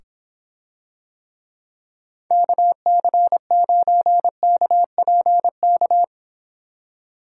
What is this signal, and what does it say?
Morse code, heard as a keyed tone, about 25 wpm, 700 Hz. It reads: KC9KPK